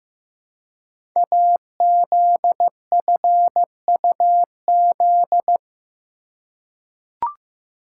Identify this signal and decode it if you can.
Morse code, heard as a keyed tone, about 15 wpm, 700 Hz. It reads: AZFUZ
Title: Code Groups - 5 Letters - 15wpm